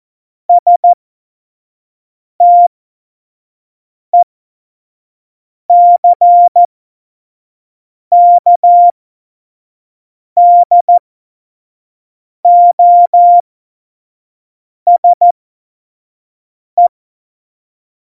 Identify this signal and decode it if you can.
Morse code, heard as a keyed tone, about 14 wpm, spaced as for 5 wpm, 700 Hz. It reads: STECKDOSE
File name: Morsekurs / Witze / Witz 24 April 2018, 25 F70